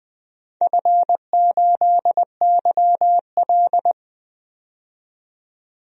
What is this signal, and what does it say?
Morse code, heard as a keyed tone, about 20 wpm, 700 Hz. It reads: F8YL
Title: DXCC-100 Call Signs - 4 Characters - 20wpm